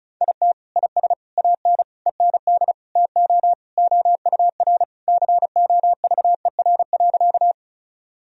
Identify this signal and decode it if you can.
Morse code, heard as a keyed tone, about 34 wpm, 700 Hz. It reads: IT IS AN END TO OUR COVER.
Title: Sentences from Top 1000 Words - Review Part 1 - 35wpm